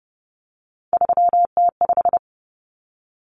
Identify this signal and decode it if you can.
Morse code, heard as a keyed tone, about 30 wpm, 700 Hz. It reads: 3T5
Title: Common Cut Numbers - 3-Digit 30wpm